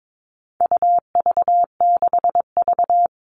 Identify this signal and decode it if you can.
Morse code, heard as a keyed tone, about 22 wpm, 700 Hz. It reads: UV6V
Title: All Cut Numbers - 4-Digit 22wpm